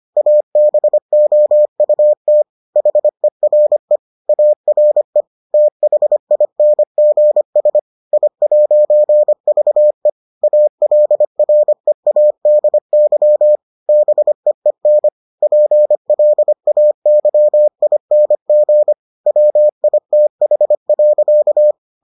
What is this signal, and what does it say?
Morse code, heard as a keyed tone, about 25 wpm, 600 Hz. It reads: ABOUT HERE ARE THINGS I'VE ALREADY BEEN PLAYING WITH.